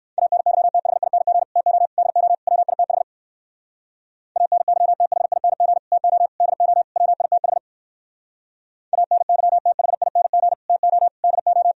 Morse code, heard as a keyed tone, about 58 wpm, 700 Hz. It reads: ANYTHING TO DO WITH ANYTHING TO DO WITH ANYTHING TO DO